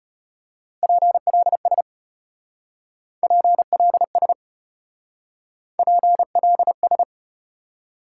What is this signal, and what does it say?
Morse code, heard as a keyed tone, about 35 wpm, 700 Hz. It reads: PLS PLS PLS